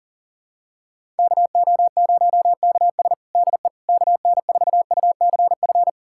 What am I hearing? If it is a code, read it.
Morse code, heard as a keyed tone, about 40 wpm, 700 Hz. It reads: KO0KS DE KN4UCF
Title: USA Callsign DE Callsign 40wpm